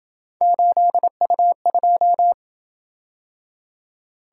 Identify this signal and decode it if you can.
Morse code, heard as a keyed tone, about 26 wpm, 700 Hz. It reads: 8U2